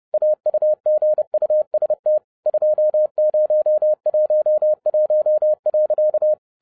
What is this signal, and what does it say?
Morse code, heard as a keyed tone, about 30 wpm, 610 Hz. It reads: AUGUST 2011.
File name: 30wpm